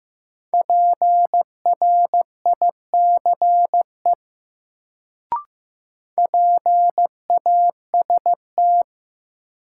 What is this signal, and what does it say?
Morse code, heard as a keyed tone, about 15 wpm, 700 Hz. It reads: PRICE PAST
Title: Sentences from Top 600 Words - New Word Drill - 15wpm